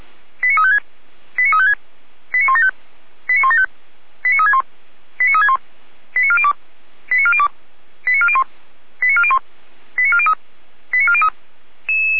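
Five-tone selective call (ZVEI), heard as quick runs of five tones, tones of about 70 ms. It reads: 89356 89356 89175 89175 89361 89361 89402 89402 89401 89401 89403 89403
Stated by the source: ZVEI five-tone selective call 89172, received on an SDR.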